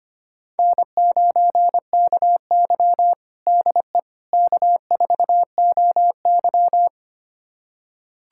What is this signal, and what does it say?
Morse code, heard as a keyed tone, about 25 wpm, 700 Hz. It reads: N9KY DE K4OY
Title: USA Callsign DE Callsign - 4 Characters - 25wpm